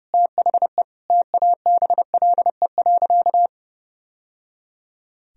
Morse code, heard as a keyed tone, about 30 wpm, 700 Hz. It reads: THE TABLE.
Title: Sentences from Top 1000 Words - Review Part 2 - 30wpm